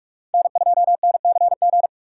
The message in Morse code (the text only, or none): N2NCG